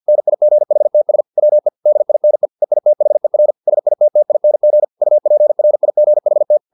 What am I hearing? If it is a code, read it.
Morse code, heard as a keyed tone, about 50 wpm, 600 Hz. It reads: NIGHTS WE DINE EITHER SITTING UPRIGHT